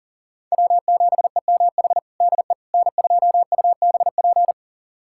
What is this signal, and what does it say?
Morse code, heard as a keyed tone, about 37 wpm, 700 Hz. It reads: W7EMH DE N2UBP
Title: USA Callsign DE Callsign - 5 Characters - 40wpm